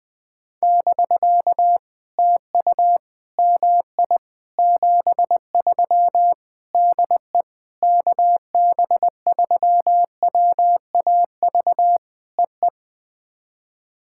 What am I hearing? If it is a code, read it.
Morse code, heard as a keyed tone, about 20 wpm, 700 Hz. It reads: <BK> TU MI 73 DE KB3WAV EE
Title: POTA QSO Practice - 20wpm